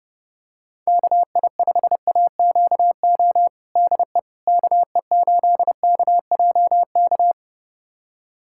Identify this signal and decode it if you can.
Morse code, heard as a keyed tone, about 32 wpm, 700 Hz. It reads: KI5AQO DE KE8KJK
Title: USA Callsign DE Callsign 30wpm